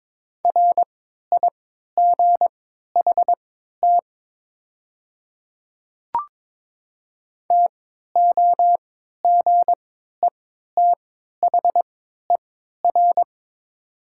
Morse code, heard as a keyed tone, about 22 wpm, spaced as for 13 wpm, 700 Hz. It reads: RIGHT TOGETHER
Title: 3x Character Spacing - Top 300 Words - 22wpm